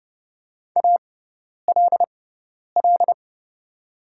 Morse code, heard as a keyed tone, about 30 wpm, spaced as for 10 wpm, 700 Hz. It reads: ALL
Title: 6x Character Spacing - Top 200 Words - 30wpm